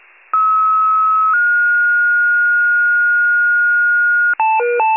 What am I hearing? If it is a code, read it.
Two-tone page, heard as a one-second tone, then a three-second tone, about 1290 then 1430 Hz. It is heard from 0.3 to 4.3 s.